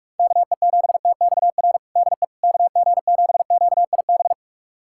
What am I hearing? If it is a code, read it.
Morse code, heard as a keyed tone, about 42 wpm, 700 Hz. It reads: KE7TXR DE KG7QIB
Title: USA Callsign DE Callsign - 6 Characters - 45wpm